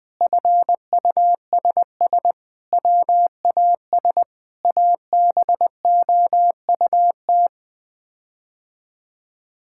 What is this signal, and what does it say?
Morse code, heard as a keyed tone, about 21 wpm, 700 Hz. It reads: FUSS WAS ABOUT
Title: Sets of 8 Words - 20wpm